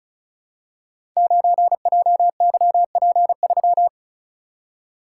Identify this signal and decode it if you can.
Morse code, heard as a keyed tone, about 36 wpm, 700 Hz. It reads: 9JYP3
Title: Code Groups - 5 Letter:Numbers - 35wpm